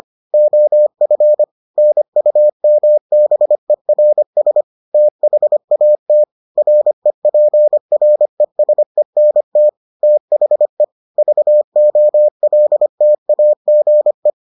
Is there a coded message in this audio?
Yes: Morse code from 0.3 to 14.3 s, heard as a keyed tone, about 25 wpm, 600 Hz. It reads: OF NUMBERS THAT REPRESENT THE VOLTAGE